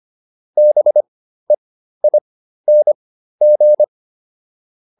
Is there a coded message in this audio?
Yes: Morse code from 0.6 to 3.8 s, heard as a keyed tone, about 25 wpm, spaced as for 13 wpm, 600 Hz. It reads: BEING